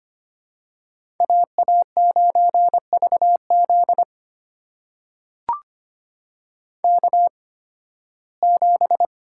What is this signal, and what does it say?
Morse code, heard as a keyed tone, about 25 wpm, spaced as for 16 wpm, 700 Hz. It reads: AA9VZ K7
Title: USA Call Signs - 5 Characters - 8x Character Spacing - 25wpm